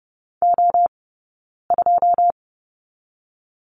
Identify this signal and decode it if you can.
Morse code, heard as a keyed tone, about 30 wpm, 700 Hz. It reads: O 2
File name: 092 Lesson 16d - Letter c - Two Characters 30wpm